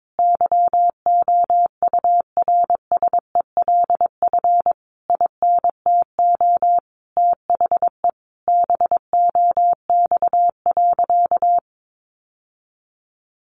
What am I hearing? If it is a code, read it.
Morse code, heard as a keyed tone, about 22 wpm, 700 Hz. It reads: YOURSELF INTO THE BOX.
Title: Sentences from Top 1000 Words 22wpm